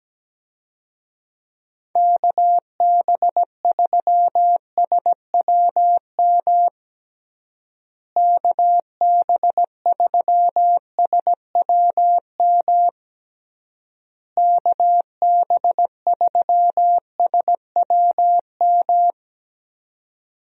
Morse code, heard as a keyed tone, about 17 wpm, 700 Hz. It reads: KB3SWM KB3SWM KB3SWM
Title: USA Call Signs - 6 Characters - Repeated 3x - 17wpm